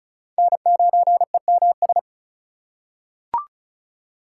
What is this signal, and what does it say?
Morse code, heard as a keyed tone, about 36 wpm, 700 Hz. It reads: N9EMS
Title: USA Call Signs - 5 Characters - 3x Character Spacing - 35wpm